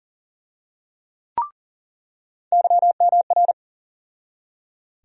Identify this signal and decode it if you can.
Morse code, heard as a keyed tone, about 37 wpm, 700 Hz. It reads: YMR
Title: USA Call Sign Postfixes - 3 Letters 40wpm